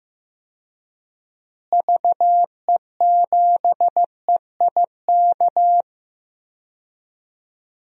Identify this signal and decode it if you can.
Morse code, heard as a keyed tone, about 16 wpm, 700 Hz. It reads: VE7EIK
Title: DXCC-100 Call Signs - 6 Characters - Repeated 3x - 15wpm